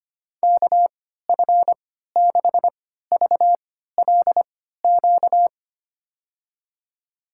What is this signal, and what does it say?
Morse code, heard as a keyed tone, about 25 wpm, spaced as for 14 wpm, 690 Hz. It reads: KF6VLQ
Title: USA Call Signs - 6 Characters - 3x Character Spacing - 25wpm